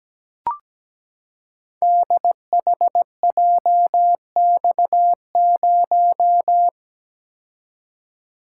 Morse code, heard as a keyed tone, about 17 wpm, 700 Hz. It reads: DHJX0
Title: Code Groups - 5 Letter:Numbers - 17wpm